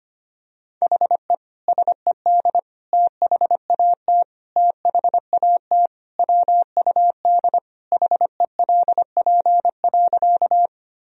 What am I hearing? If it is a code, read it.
Morse code, heard as a keyed tone, about 25 wpm, 700 Hz. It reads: HE SED THAT THAT WUD HELP.
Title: Sentences from Top 200 Words and Abbreviations 25wpm